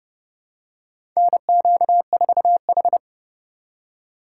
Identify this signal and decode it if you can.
Morse code, heard as a keyed tone, about 30 wpm, 700 Hz. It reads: NQ4H
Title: USA Call Signs - 4 Characters - 6x Character Spacing - 30wpm